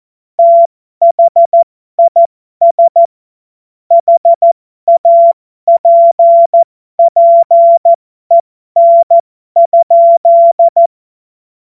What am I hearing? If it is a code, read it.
Morse code, heard as a keyed tone, about 14 wpm, spaced as for 12 wpm, 680 Hz. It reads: THIS HAPPEN?
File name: Morsekurs / Witze